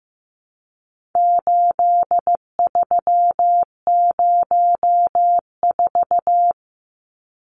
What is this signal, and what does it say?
Morse code, heard as a keyed tone, about 15 wpm, 700 Hz. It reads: 8304
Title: Numbers - 4-Digit 15wpm